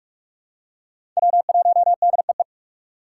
Morse code, heard as a keyed tone, about 46 wpm, 700 Hz. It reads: W1DEE
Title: USA Call Signs - 5 Characters - 2x Character Spacing - 45wpm